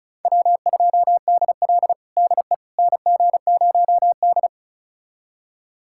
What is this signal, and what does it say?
Morse code, heard as a keyed tone, about 34 wpm, 700 Hz. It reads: W2DL DE NG0D
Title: USA Callsign DE Callsign - 4 Characters - 35wpm